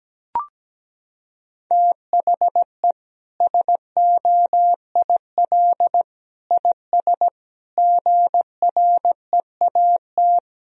Morse code, heard as a keyed tone, about 17 wpm, 700 Hz. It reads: THE SOIL IS GREAT
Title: Sentences from Top 1000 Words - Review Part 4 - 17wpm